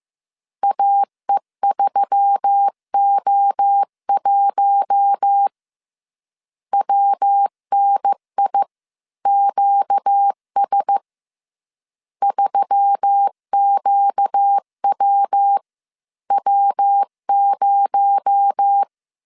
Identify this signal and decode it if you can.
Morse code, heard as a keyed tone, about 15 wpm, 800 Hz. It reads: AE3O1 WNIQS 3QWW0